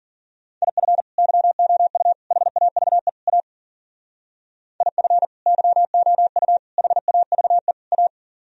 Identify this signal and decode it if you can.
Morse code, heard as a keyed tone, about 52 wpm, 700 Hz. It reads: IF YOU HAVE A IF YOU HAVE A